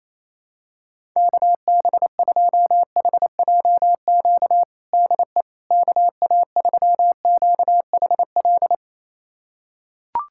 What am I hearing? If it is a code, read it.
Morse code, heard as a keyed tone, about 28 wpm, 700 Hz. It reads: KB2HJQ DE KA3QHL
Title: USA Callsign DE Callsign 28wpm